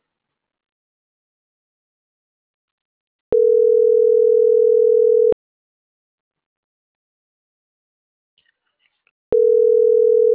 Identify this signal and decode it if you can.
Ringback tone (North America), 440 + 480 Hz.